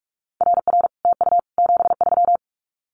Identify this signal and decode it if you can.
Morse code, heard as a keyed tone, about 46 wpm, 700 Hz. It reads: RR TU 73
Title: Common QSO Phrases Encore 45wpm